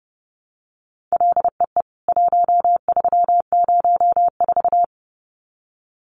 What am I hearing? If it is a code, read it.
Morse code, heard as a keyed tone, about 28 wpm, 700 Hz. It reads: LEE 1304
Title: CWT Exchange 30wpm